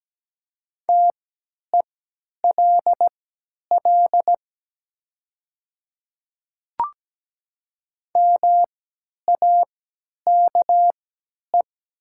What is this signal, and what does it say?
Morse code, heard as a keyed tone, about 17 wpm, spaced as for 10 wpm, 700 Hz. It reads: TELL MAKE
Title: 3x Character Spacing - Top 200 Words - 17wpm